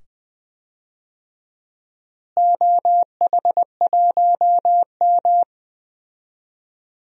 Morse code, heard as a keyed tone, about 20 wpm, 700 Hz. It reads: OH1M
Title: DXCC-100 Call Signs - 4 Characters - Repeated 2x - 20wpm